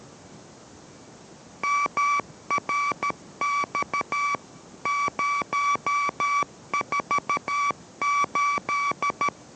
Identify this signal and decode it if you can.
Morse code, heard as a keyed tone, about 13 wpm, 1150 Hz. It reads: MRX 048